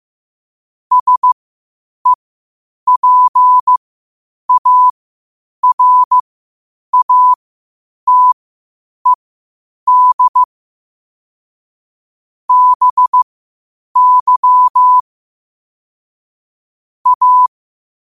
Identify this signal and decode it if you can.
Morse code, heard as a keyed tone, about 15 wpm, spaced as for 8 wpm, 1000 Hz. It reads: SEPARATED BY A
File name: qotdfiles / 8wpm